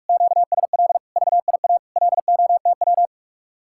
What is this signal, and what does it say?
Morse code, heard as a keyed tone, about 49 wpm, 700 Hz. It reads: QSL VIA LOTW